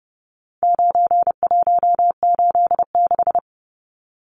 Morse code, heard as a keyed tone, about 30 wpm, 700 Hz. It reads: 9186
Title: Numbers - 4-Digit 30wpm